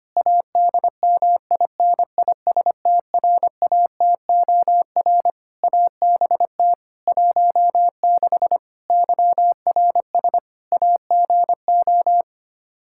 Morse code, heard as a keyed tone, about 25 wpm, 700 Hz. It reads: ADMINISTRATOR ABT 16 YRS AGO
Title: Practice QSO - Exchange 3 - More Info - 25wpm